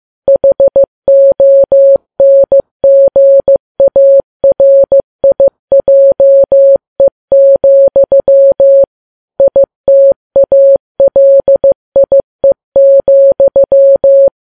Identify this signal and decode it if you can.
Morse code, heard as a keyed tone, about 15 wpm, 550 Hz. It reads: HONGARIJE, ITALIE,
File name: Files Web tls / Morse on9fv / morse15kps